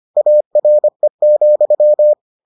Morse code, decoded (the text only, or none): ARE,